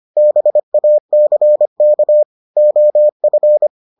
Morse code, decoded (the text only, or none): BACK OF